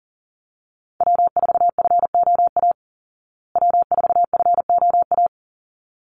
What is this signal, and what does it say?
Morse code, heard as a keyed tone, about 40 wpm, 700 Hz. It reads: W4FOA W4FOA
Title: Call Signs Repeated 3x - 40wpm